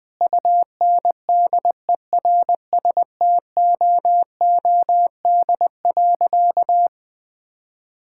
Morse code, heard as a keyed tone, about 20 wpm, 700 Hz. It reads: UNDERSTOOD.
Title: Sentences with 2-Word Phrases - 20wpm